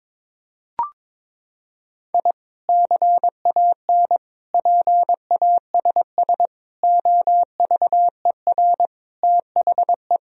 Morse code, heard as a keyed tone, about 22 wpm, 700 Hz. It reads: I CAN PASS OVER THE
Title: Sentences from Top 1000 Words - Review Part 1 - 22wpm